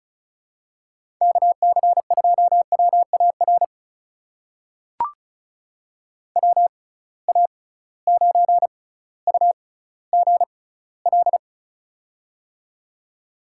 Morse code, heard as a keyed tone, about 36 wpm, spaced as for 18 wpm, 700 Hz. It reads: KC2WAR WA9UGL
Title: USA Call Signs - 6 Characters - 6x Character Spacing - 35wpm